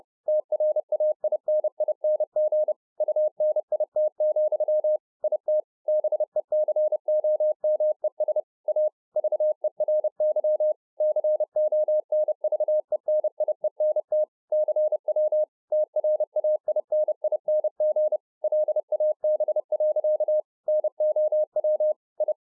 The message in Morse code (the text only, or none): TRAINING UNIT, IT BECOMES A VERY CONVENIENT CW TRAINING LAB. NOW I